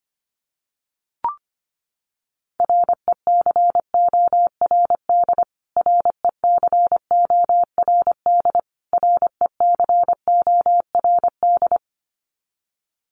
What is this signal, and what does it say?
Morse code, heard as a keyed tone, about 25 wpm, 700 Hz. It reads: RECORD RECORD RECORD
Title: Top 500 Words Repeated 25wpm